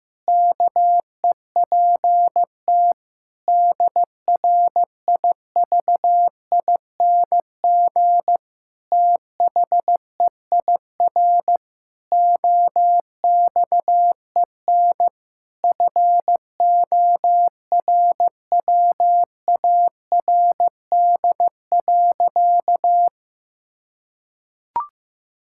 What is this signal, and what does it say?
Morse code, heard as a keyed tone, about 15 wpm, 700 Hz. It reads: KEPT DRIVING THEIR OXEN FORWARD.
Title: Pangram Challenge v2 - 15wpm